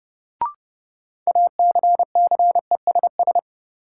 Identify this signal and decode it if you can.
Morse code, heard as a keyed tone, about 30 wpm, 700 Hz. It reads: ACCESS
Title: Sentences from Top 1000 Words - Review Part 4 - New Word Drill - 30wpm